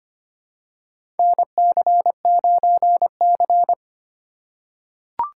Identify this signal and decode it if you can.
Morse code, heard as a keyed tone, about 25 wpm, 700 Hz. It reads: NC9C
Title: USA Call Signs - 4 Characters - 8x Character Spacing - 25wpm